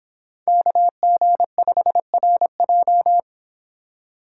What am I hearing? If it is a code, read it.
Morse code, heard as a keyed tone, about 30 wpm, 700 Hz. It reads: KG5RJ